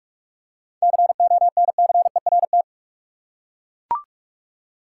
Morse code, heard as a keyed tone, about 42 wpm, 700 Hz. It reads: CONCERT
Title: Seven Letter Words - 45wpm